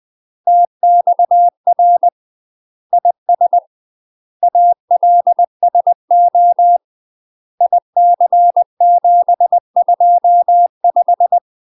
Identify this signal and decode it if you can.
Morse code, heard as a keyed tone, about 20 wpm, 700 Hz. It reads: TXR IS ALSO IC725
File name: UK MASTER QSOs TRAINING / 20WPM / UK MASTER QSOs 0044